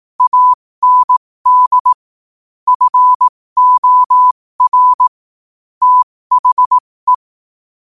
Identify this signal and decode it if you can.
Morse code, heard as a keyed tone, about 18 wpm, spaced as for 15 wpm, 1000 Hz. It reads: AND FOR THE
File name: qotdfiles / 15wpm